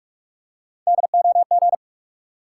Morse code, heard as a keyed tone, about 44 wpm, 700 Hz. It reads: DOG
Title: Three Letter Words - 45wpm